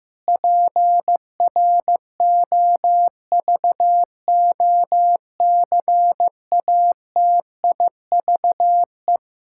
Morse code, heard as a keyed tone, about 16 wpm, 700 Hz. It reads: PROVOCATIVE
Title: Pangram Challenge v2 - 15wpm